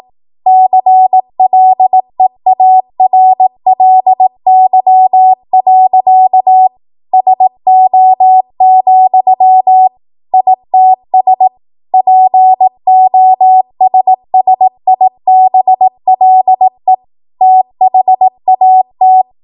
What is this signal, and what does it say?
Morse code, heard as a keyed tone, about 18 wpm, 750 Hz. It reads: CLEARLY. SO, ITS POSSIBLE THAT
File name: Morse / 18WPM